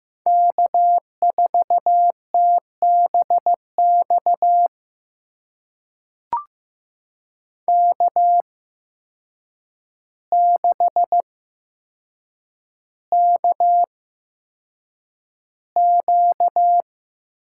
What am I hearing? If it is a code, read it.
Morse code, heard as a keyed tone, about 15 wpm, spaced as for 7 wpm, 700 Hz. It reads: K4TBX K6KQ
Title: USA Call Signs - 8x Character Spacing - 15wpm